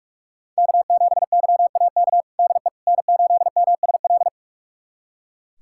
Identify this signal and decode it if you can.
Morse code, heard as a keyed tone, about 43 wpm, 700 Hz. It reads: K7YAK DE N8MSL
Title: USA Callsign DE Callsign - 5 Characters - 45wpm